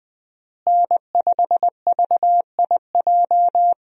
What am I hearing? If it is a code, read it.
Morse code, heard as a keyed tone, about 20 wpm, 700 Hz. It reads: N5VIJ
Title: USA Call Signs - 5 Characters - Repeated 2x - 20wpm